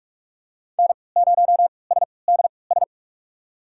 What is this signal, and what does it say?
Morse code, heard as a keyed tone, about 43 wpm, spaced as for 25 wpm, 700 Hz. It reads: N0SDS
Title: USA Call Signs - 3x Character Spacing - 45wpm